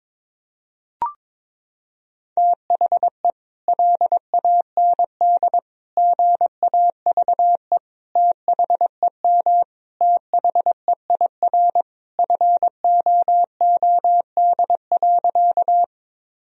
Morse code, heard as a keyed tone, about 22 wpm, 700 Hz. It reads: THE LAND GAVE THEM THEIR FOOD.
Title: Sentences from Top 1000 Words - Review Part 1 - 22wpm